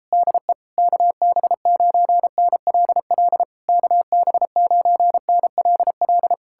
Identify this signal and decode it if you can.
Morse code, heard as a keyed tone, about 38 wpm, 700 Hz. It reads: DE KB9NLL KB9NLL